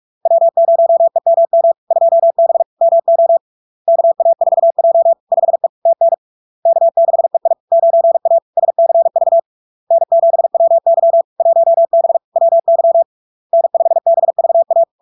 Morse code, heard as a keyed tone, about 46 wpm, 670 Hz. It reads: W0EMM 2B MO KA4J 5E TN K6EI 9A SCV N7WY 1B WY N5BVA
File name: LARK-FD-2022-Log-in-CW-45wpm-670-Hz